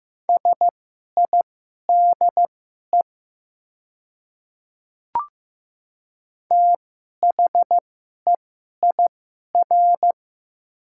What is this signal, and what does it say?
Morse code, heard as a keyed tone, about 15 wpm, spaced as for 11 wpm, 700 Hz. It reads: SIDE THEIR
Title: 2x Character Spacing - Top 100 Words - 15wpm